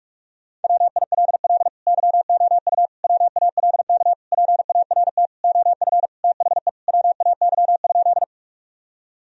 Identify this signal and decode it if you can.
Morse code, heard as a keyed tone, about 41 wpm, 700 Hz. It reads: WILL YOU WALK PART OF THE WAY?
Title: Sentences from Top 1000 Words - Review Part 1 - 45wpm